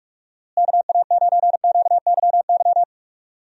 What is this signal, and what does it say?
Morse code, heard as a keyed tone, about 43 wpm, 700 Hz. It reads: KA9QYY
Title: USA Call Signs - 6 Characters - Repeated 2x - 45wpm